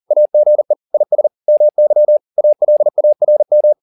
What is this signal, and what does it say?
Morse code, heard as a keyed tone, about 40 wpm, 600 Hz. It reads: AGE IS MY ALARM